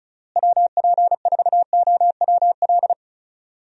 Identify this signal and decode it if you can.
Morse code, heard as a keyed tone, about 36 wpm, 700 Hz. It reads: WP4OWL
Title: USA Call Signs - 6 Characters - Repeated 3x - 35wpm